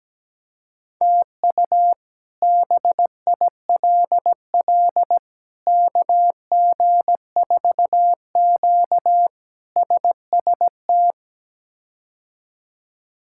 Morse code, heard as a keyed tone, about 17 wpm, 700 Hz. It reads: TU BILL KG4Q SST
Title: K1USN SST QSO Practice - 17wpm